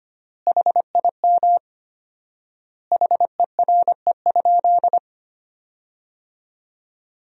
Morse code, heard as a keyed tone, about 25 wpm, 700 Hz. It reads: HIM HERE?
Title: 4x Word-Spacing - Sentences from Top 500 Words - 25wpm